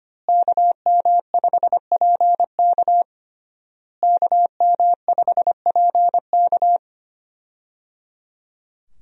Morse code, heard as a keyed tone, about 25 wpm, 700 Hz. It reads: KM5PK KM5PK
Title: USA Call Signs - 5 Characters - Repeated 3x - 25wpm